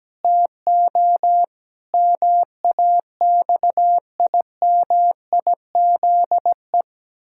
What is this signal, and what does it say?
Morse code, heard as a keyed tone, about 17 wpm, 700 Hz. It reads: TO MAXIMIZE
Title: Pangram Challenge v2 - 17wpm